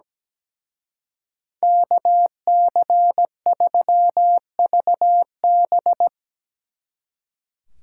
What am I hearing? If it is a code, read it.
Morse code, heard as a keyed tone, about 17 wpm, 700 Hz. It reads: KC3VB
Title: Speed Racing - USA 2x2 Call Signs 17wpm